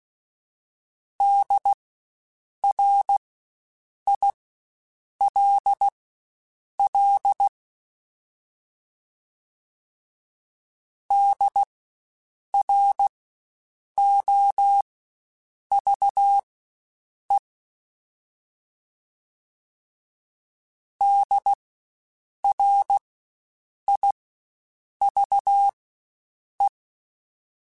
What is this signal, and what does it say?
Morse code, heard as a keyed tone, about 16 wpm, spaced as for 7 wpm, 800 Hz. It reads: DRILL DROVE DRIVE
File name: Spaced Alphas / Spaced Alphas mp3s / Ds sp @15